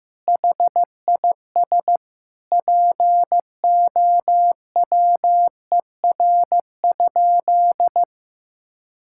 Morse code, heard as a keyed tone, about 15 wpm, 700 Hz. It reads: HIS POWER?
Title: Sentences from Top 1000 Words - Review Part 5 - 15wpm